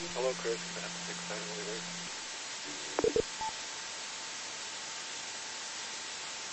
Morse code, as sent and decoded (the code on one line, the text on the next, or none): none